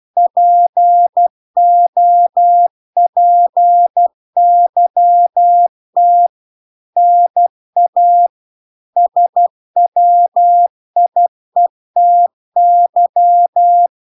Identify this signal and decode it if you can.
Morse code, heard as a keyed tone, about 12 wpm, 700 Hz. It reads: POPYT NA SWIETY